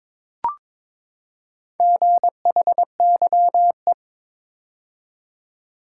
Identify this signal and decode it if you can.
Morse code, heard as a keyed tone, about 22 wpm, 700 Hz. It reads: GHYE
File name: practice / 22wpm / Four Letter-Number Combinations - 22wpm